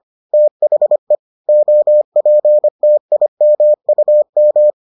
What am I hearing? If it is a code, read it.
Morse code, heard as a keyed tone, about 25 wpm, 600 Hz. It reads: THE OPTIMUM